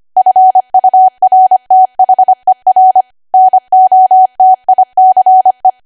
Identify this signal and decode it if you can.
Morse code, heard as a keyed tone, about 25 wpm, 750 Hz. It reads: FURTHER NOTICE
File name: cwa / W1AW-Level3 / 25WPM